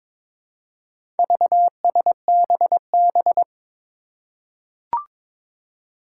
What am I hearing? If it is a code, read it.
Morse code, heard as a keyed tone, about 22 wpm, 700 Hz. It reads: VSBB
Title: Four Challenging Letter-Number Combinations 22wpm